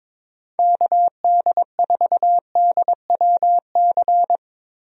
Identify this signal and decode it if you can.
Morse code, heard as a keyed tone, about 22 wpm, 700 Hz. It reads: KD4DWC
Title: USA Call Signs - 6 Characters - Repeated 3x - 22wpm